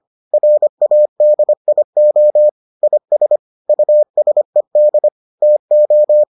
Morse code, heard as a keyed tone, about 25 wpm, 600 Hz. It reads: RADIO IS USED TO